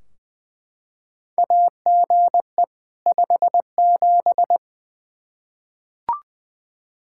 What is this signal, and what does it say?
Morse code, heard as a keyed tone, about 20 wpm, 700 Hz. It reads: AGE 57
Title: Practice QSO - Exchange 3 - More Info - 20wpm